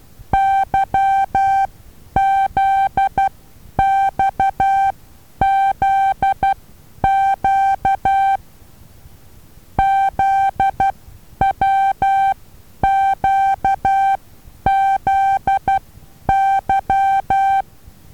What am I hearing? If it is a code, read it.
Morse code, heard as a keyed tone, about 12 wpm, spaced as for 9 wpm, 780 Hz. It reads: YZXZQ ZWQZY
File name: Taviro HA6VA B